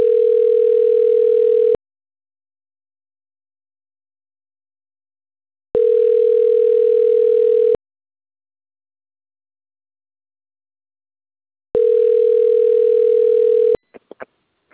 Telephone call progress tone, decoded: ringback tone